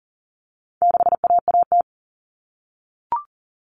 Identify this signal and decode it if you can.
Morse code, heard as a keyed tone, about 40 wpm, 700 Hz. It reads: 6AAT
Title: Common Cut Numbers - 4-Digit 40wpm